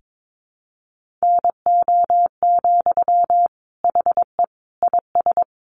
Morse code, heard as a keyed tone, about 22 wpm, 700 Hz. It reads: NO, HE IS
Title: Sentences from Top 1400 Words 22wpm